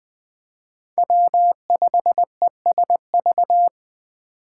Morse code, heard as a keyed tone, about 20 wpm, 700 Hz. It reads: W5ESV
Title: Speed Racing - USA Call Signs 17wpm